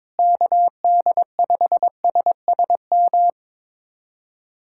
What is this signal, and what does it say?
Morse code, heard as a keyed tone, about 22 wpm, 700 Hz. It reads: KD5SSM
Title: USA Callsign DE Callsign - 6 Characters - 22wpm